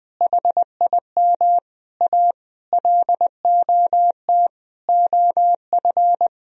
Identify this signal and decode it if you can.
Morse code, heard as a keyed tone, about 20 wpm, 700 Hz. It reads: HIM A LOT OF
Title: Sentences from Top 1000 Words - Review Part 3 - 20wpm